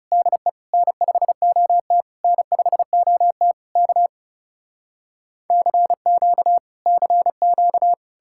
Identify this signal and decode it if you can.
Morse code, heard as a keyed tone, about 40 wpm, 700 Hz. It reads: DE N5OT N5OT K CQ CQ